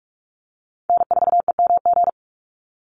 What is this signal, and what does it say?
Morse code, heard as a keyed tone, about 44 wpm, 690 Hz. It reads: N4EMG
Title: NA SOTA Chaser Callsigns 45wpm